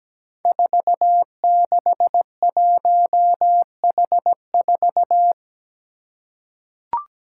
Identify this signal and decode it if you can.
Morse code, heard as a keyed tone, about 17 wpm, 700 Hz. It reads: 461H4
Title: Five Challenging Letter-Number Combinations 17wpm